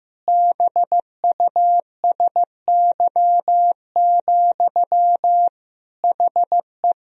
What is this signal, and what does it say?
Morse code, heard as a keyed tone, about 16 wpm, 700 Hz. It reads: BUSY, HE
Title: Sentences from Top 1000 Words - Review Part 3 - 15wpm